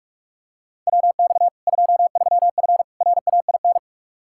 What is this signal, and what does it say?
Morse code, heard as a keyed tone, about 46 wpm, 700 Hz. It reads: WX 23F RAIN
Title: Practice QSO - Exchange 2 - Info - 45wpm